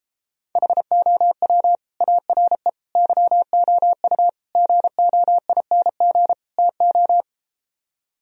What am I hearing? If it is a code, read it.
Morse code, heard as a keyed tone, about 33 wpm, 700 Hz. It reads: HOW ARE YOU GOING TO